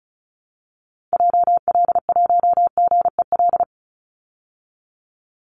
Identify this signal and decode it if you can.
Morse code, heard as a keyed tone, about 34 wpm, 700 Hz. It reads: JL1GEL
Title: Call Signs Repeated 3x - 35wpm